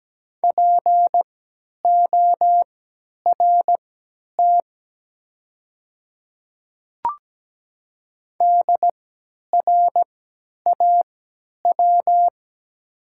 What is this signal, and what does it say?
Morse code, heard as a keyed tone, about 17 wpm, spaced as for 10 wpm, 700 Hz. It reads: PORT DRAW
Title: 3x Character Spacing - Top 300 Words - 17wpm